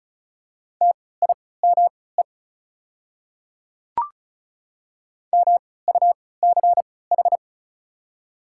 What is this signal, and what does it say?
Morse code, heard as a keyed tone, about 34 wpm, spaced as for 20 wpm, 700 Hz. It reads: TIME MUCH
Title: 3x Character Spacing - Top 300 Words - 35wpm